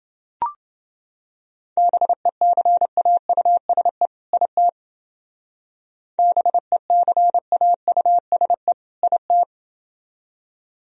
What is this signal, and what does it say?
Morse code, heard as a keyed tone, about 29 wpm, 700 Hz. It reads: BECAUSE IT BECAUSE IT